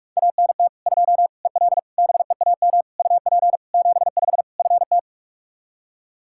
Morse code, heard as a keyed tone, about 46 wpm, 700 Hz. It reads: ANT 2 EL BEAM UP 75 FT
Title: Practice QSO - Exchange 2 - Info - 45wpm